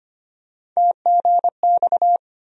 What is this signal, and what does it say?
Morse code, heard as a keyed tone, about 25 wpm, 700 Hz. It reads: TGX
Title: USA Call Sign Postfixes - 3 Letters 25wpm